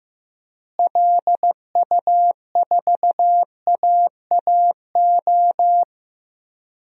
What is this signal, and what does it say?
Morse code, heard as a keyed tone, about 15 wpm, 700 Hz. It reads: LU4AAO
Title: DXCC-100 Call Signs - 6 Characters - Repeated 3x - 15wpm